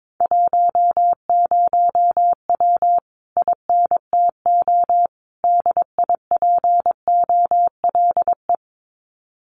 Morse code, heard as a keyed tone, about 22 wpm, 700 Hz. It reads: 10W INTO DIPOLE
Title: Common QSO Phrases Encore 22wpm